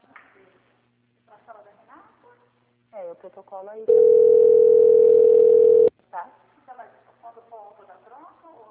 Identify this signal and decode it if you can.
Ringback tone (North America), 440 + 480 Hz, with speech.
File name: out- - - - - .62610